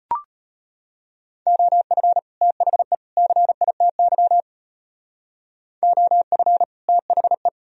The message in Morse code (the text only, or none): OF THE CITY OF THE